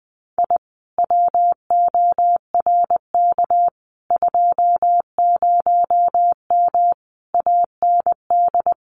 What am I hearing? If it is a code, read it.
Morse code, heard as a keyed tone, about 20 wpm, 700 Hz. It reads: I WORK 20M AND
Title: Common QSO Phrases Encore 20wpm